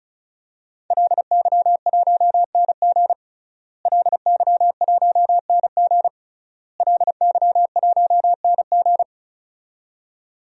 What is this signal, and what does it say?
Morse code, heard as a keyed tone, about 36 wpm, 700 Hz. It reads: LY1NG LY1NG LY1NG
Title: DXCC-100 Call Signs - 5 Characters - Repeated 3x - 35wpm